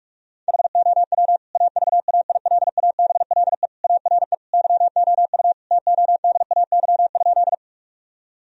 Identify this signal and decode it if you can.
Morse code, heard as a keyed tone, about 46 wpm, 700 Hz. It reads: HOW AVAILABLE ARE YOU TODAY?
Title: Sentences from Top 1000 Words - Review Part 3 - 45wpm